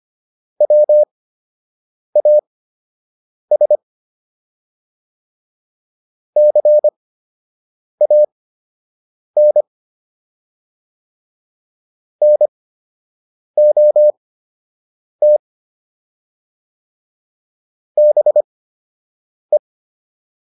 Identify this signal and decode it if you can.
Morse code, heard as a keyed tone, about 25 wpm, spaced as for 7 wpm, 600 Hz. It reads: WAS CAN NOT BE